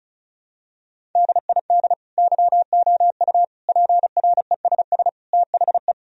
Morse code, heard as a keyed tone, about 36 wpm, 700 Hz. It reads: DID YOU PRESS THE
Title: Sentences from Top 1000 Words - Review Part 1 - 35wpm